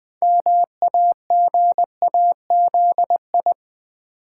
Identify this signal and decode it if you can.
Morse code, heard as a keyed tone, about 20 wpm, 700 Hz. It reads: MAGAZI
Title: Build A Word - 3 Letters At A Time - 8-Letter Words - 20wpm